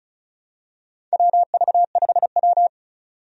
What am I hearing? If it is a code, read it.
Morse code, heard as a keyed tone, about 34 wpm, 700 Hz. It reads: WV5W
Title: USA Call Signs - 4 Characters - 35wpm